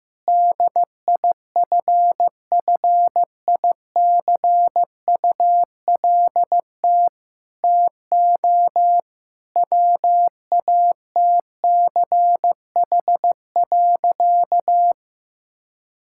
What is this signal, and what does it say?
Morse code, heard as a keyed tone, about 15 wpm, 700 Hz. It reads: DIFFICULT TO WATCH.
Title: Sentences from Top 1000 Words - Review Part 5 - 15wpm